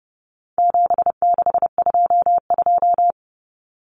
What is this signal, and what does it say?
Morse code, heard as a keyed tone, about 30 wpm, 700 Hz. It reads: 7622
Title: Common Cut Numbers - 4-Digit 30wpm